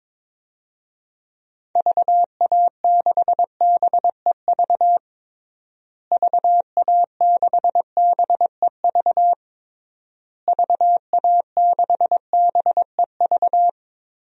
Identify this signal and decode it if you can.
Morse code, heard as a keyed tone, about 22 wpm, 700 Hz. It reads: VA6BEV VA6BEV VA6BEV
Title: DXCC-100 Call Signs - 6 Characters - Repeated 3x - 22wpm